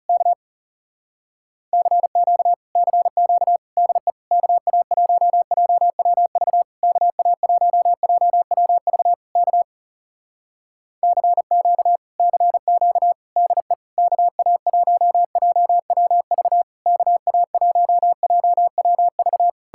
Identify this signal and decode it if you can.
Morse code, heard as a keyed tone, about 37 wpm, 700 Hz. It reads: K CQ CQ DE KA1JWV KA1JWV K CQ CQ DE KA1JWV KA1JWV
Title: Speed-Racing Common QSO Phrases - 30wpm